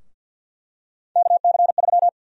Morse code, heard as a keyed tone, about 50 wpm, 700 Hz. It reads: KC3
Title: USA Call Sign Prefixes - Repeated 2x - 50wpm